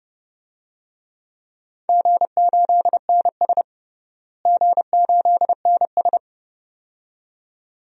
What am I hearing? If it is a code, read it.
Morse code, heard as a keyed tone, about 30 wpm, 700 Hz. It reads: G8NS G8NS
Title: DXCC-100 Call Signs - 4 Characters - Repeated 2x - 30wpm